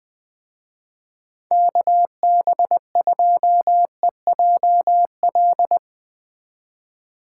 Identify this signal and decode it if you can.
Morse code, heard as a keyed tone, about 20 wpm, 700 Hz. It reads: KB2EJL
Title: Speed Racing - USA 2x3 Call Signs 17wpm